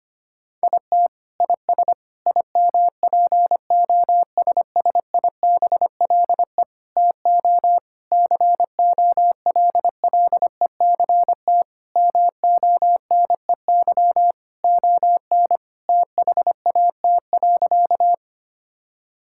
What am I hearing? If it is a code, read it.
Morse code, heard as a keyed tone, about 25 wpm, 700 Hz. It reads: IT IS IMPOSSIBLE TO COLLECT MONEY ON THAT.
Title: Sentences from Top 1600 Words 25wpm